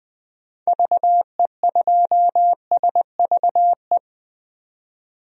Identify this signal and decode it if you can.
Morse code, heard as a keyed tone, about 20 wpm, 700 Hz. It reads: VE2SVE
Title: DXCC-100 Call Signs - 6 Characters - 20wpm